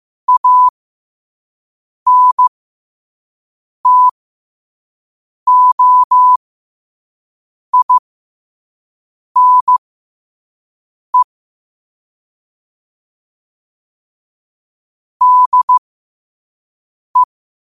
Morse code, heard as a keyed tone, about 15 wpm, spaced as for 5 wpm, 1000 Hz. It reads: ANTOINE DE
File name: qotdfiles / 5wpm